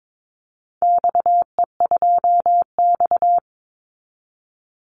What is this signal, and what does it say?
Morse code, heard as a keyed tone, about 22 wpm, 700 Hz. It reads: XE2X
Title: Call Signs Repeated 3x - 22wpm